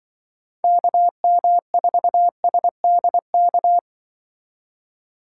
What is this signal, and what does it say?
Morse code, heard as a keyed tone, about 25 wpm, 700 Hz. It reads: KM4SDK